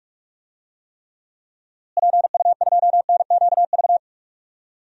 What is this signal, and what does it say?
Morse code, heard as a keyed tone, about 46 wpm, 700 Hz. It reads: PU2NQV
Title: DXCC-100 Call Signs - 6 Characters - Repeated 3x - 45wpm